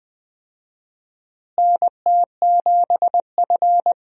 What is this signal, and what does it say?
Morse code, heard as a keyed tone, about 20 wpm, 700 Hz. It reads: NT7F
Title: USA Call Signs - 4 Characters - Repeated 3x - 20wpm